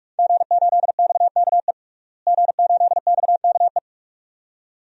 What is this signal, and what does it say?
Morse code, heard as a keyed tone, about 42 wpm, 700 Hz. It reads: G8XKE G8XKE
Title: DXCC-100 Call Signs - 5 Characters - Repeated 2x - 45wpm